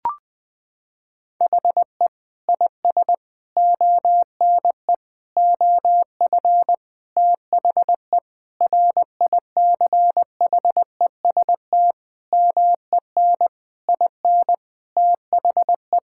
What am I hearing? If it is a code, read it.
Morse code, heard as a keyed tone, about 20 wpm, 700 Hz. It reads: HE IS ONE OF THE RICHEST MEN IN THE
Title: Sentences with 8-Word Phrases - 20wpm